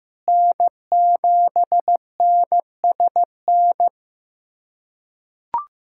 Morse code, heard as a keyed tone, about 16 wpm, 700 Hz. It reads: N7NSN
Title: USA Callsign DE Callsign - 5 Characters - 15wpm